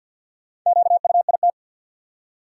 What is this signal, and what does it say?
Morse code, heard as a keyed tone, about 46 wpm, 700 Hz. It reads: QUIT
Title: Four Letter Words - 50wpm